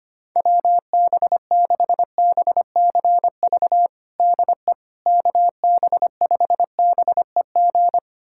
Morse code, heard as a keyed tone, about 25 wpm, 700 Hz. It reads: WB6BCV DE KB5BEG
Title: USA Callsign DE Callsign - 6 Characters - 25wpm